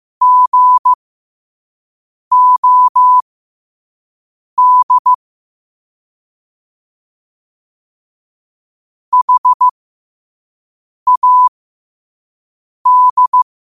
Morse code, heard as a keyed tone, about 15 wpm, spaced as for 5 wpm, 1000 Hz. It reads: GOD HAD
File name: qotdfiles / 5wpm